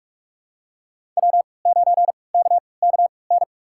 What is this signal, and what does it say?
Morse code, heard as a keyed tone, about 46 wpm, spaced as for 26 wpm, 700 Hz. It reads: W9KKN
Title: USA Call Signs - 5 Characters - 3x Character Spacing - 45wpm